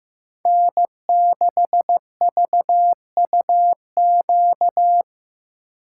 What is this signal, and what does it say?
Morse code, heard as a keyed tone, about 15 wpm, 700 Hz. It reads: N6VUQ
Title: USA Call Signs - 5 Characters - 8x Character Spacing - 15wpm